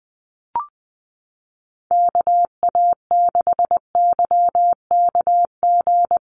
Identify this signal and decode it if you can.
Morse code, heard as a keyed tone, about 20 wpm, 700 Hz. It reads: KA6YKG
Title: USA Call Signs Repeated 2x 20wpm